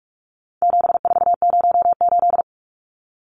Morse code, heard as a keyed tone, about 44 wpm, 700 Hz. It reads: 7408
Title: Numbers - 4-Digit 45wpm